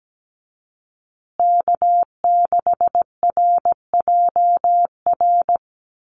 Morse code, heard as a keyed tone, about 17 wpm, 700 Hz. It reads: K6RJR
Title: USA Call Signs Repeated 2x 17wpm